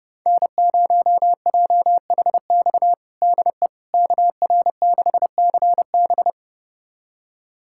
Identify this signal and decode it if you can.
Morse code, heard as a keyed tone, about 31 wpm, 700 Hz. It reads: N0JHX DE KR6CB
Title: USA Callsign DE Callsign - 5 Characters - 30wpm